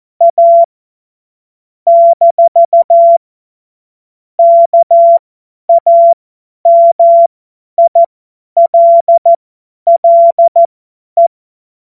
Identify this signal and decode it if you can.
Morse code, heard as a keyed tone, about 14 wpm, spaced as for 10 wpm, 680 Hz. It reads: A - KAMILLE
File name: Morsekurs / Witze / Witz 30 Januar 2024 50 F70